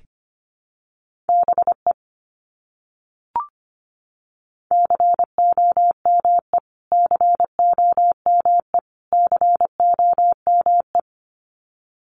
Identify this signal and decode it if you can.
Morse code, heard as a keyed tone, about 25 wpm, 700 Hz. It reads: BE COME COME COME
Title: Top 200 Words Repeated 25wpm